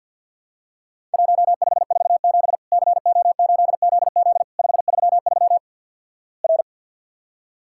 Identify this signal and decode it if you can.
Morse code, heard as a keyed tone, about 50 wpm, 700 Hz. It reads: 1547 CO8ZZ 533 R
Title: Worked All Europe DX Contest - QTC Exchange with Repeats - 50wpm